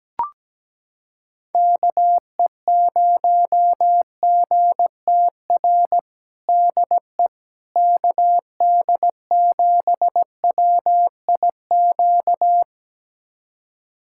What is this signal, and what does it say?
Morse code, heard as a keyed tone, about 17 wpm, 700 Hz. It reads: KE0GTR DE KD7WIQ
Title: USA Callsign DE Callsign - 6 Characters - 17wpm